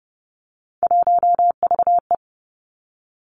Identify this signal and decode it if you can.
Morse code, heard as a keyed tone, about 30 wpm, 700 Hz. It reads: 1VE
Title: Three Challenging Letter-Number Combinations 30wpm